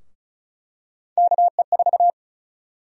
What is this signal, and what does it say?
Morse code, heard as a keyed tone, about 36 wpm, 700 Hz. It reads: KE4
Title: USA Call Sign Prefixes - 35wpm